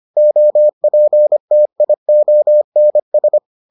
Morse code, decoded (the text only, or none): OPTIONS